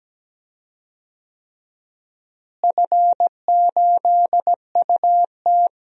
Morse code, heard as a keyed tone, about 17 wpm, 700 Hz. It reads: F8UT
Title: DXCC-100 Call Signs - 4 Characters - Repeated 3x - 17wpm